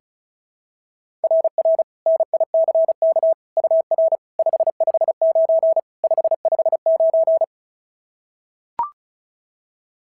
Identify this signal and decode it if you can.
Morse code, heard as a keyed tone, about 36 wpm, 690 Hz. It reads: RR NICK UR 559 559
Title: SOTA QSO Practice 35wpm